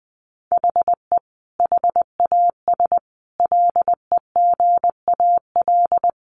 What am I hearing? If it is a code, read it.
Morse code, heard as a keyed tone, about 20 wpm, 700 Hz. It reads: HE HAS LEGAL
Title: Sentences from Top 1200 Words 20wpm